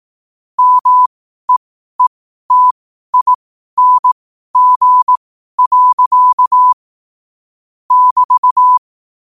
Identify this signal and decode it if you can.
Morse code, heard as a keyed tone, about 18 wpm, spaced as for 12 wpm, 1000 Hz. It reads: MEETING. <BT>